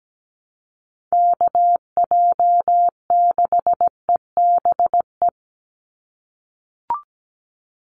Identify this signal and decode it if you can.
Morse code, heard as a keyed tone, about 17 wpm, 700 Hz. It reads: KJ6EBE
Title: USA Call Signs 17wpm